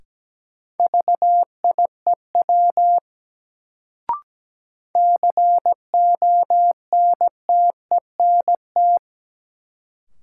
Morse code, heard as a keyed tone, about 17 wpm, 700 Hz. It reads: VIEW CONTENT
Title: Sentences from Top 900 Words - New Word Drill - 17wpm